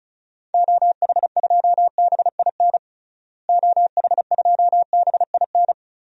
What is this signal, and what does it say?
Morse code, heard as a keyed tone, about 36 wpm, 700 Hz. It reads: OH2BIN OH2BIN
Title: DXCC-100 Call Signs - 6 Characters - Repeated 3x - 35wpm